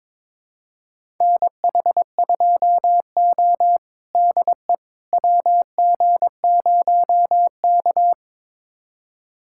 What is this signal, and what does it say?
Morse code, heard as a keyed tone, about 22 wpm, 700 Hz. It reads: NH2O DE WG0K
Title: USA Callsign DE Callsign - 4 Characters - 22wpm